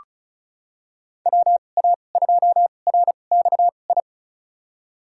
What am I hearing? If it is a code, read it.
Morse code, heard as a keyed tone, about 36 wpm, spaced as for 25 wpm, 700 Hz. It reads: WA2RXI
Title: USA Call Signs - 6 Characters - 2x Character Spacing - 35wpm